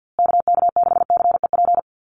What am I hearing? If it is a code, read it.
Morse code, heard as a keyed tone, about 50 wpm, 700 Hz. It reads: KK6CEP
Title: USA Call Signs Repeated 3x 50wpm